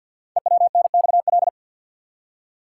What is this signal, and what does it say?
Morse code, heard as a keyed tone, about 48 wpm, 700 Hz. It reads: EWNXL
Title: Code Groups - 5 Letters - 50wpm